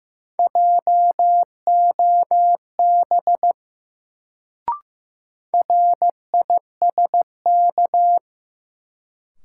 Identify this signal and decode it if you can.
Morse code, heard as a keyed tone, about 15 wpm, 700 Hz. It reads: JOB RISK
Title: Sentences from Top 600 Words - New Word Drill - 15wpm